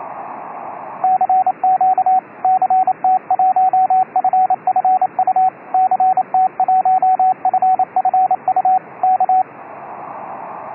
Morse code, heard as a keyed tone, about 27 wpm, 750 Hz, with radio noise. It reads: CQ CT1FFU CT1FFU K